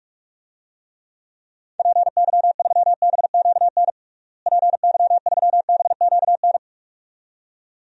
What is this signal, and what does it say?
Morse code, heard as a keyed tone, about 46 wpm, 700 Hz. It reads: PY3BQN PY3BQN
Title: DXCC-100 Call Signs - 6 Characters - Repeated 2x - 45wpm